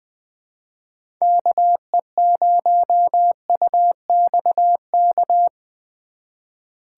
Morse code, heard as a keyed tone, about 20 wpm, 700 Hz. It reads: KE0UXK
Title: USA Call Signs - 6 Characters - 2x Character Spacing - 20wpm